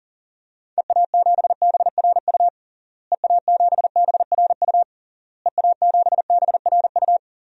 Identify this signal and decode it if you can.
Morse code, heard as a keyed tone, about 40 wpm, 700 Hz. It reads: EA7BRU EA7BRU EA7BRU
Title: DXCC-100 Call Signs - 6 Characters - Repeated 3x - 40wpm